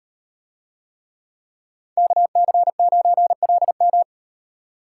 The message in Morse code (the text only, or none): KC9LM